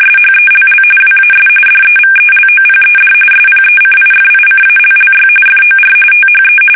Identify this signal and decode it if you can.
RTTY, about 45.45 baud. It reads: SHELL WAS DEVELOPED SPECIFICALLY FOR THE
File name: fsk850 / battleshipnj